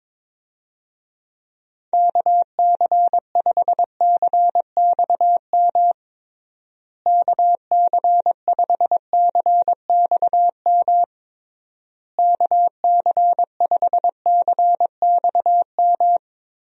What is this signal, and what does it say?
Morse code, heard as a keyed tone, about 22 wpm, 700 Hz. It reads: KC5CXM KC5CXM KC5CXM
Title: USA Call Signs - 6 Characters - Repeated 3x - 22wpm